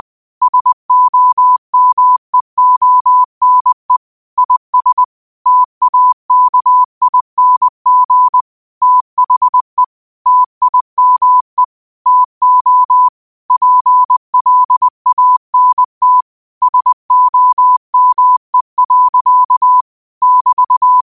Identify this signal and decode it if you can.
Morse code, heard as a keyed tone, about 20 wpm, 1000 Hz. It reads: SOMEONE IS TAKING THE TIME TO PLANT SOME. <BT>
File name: qotdfiles / 20wpm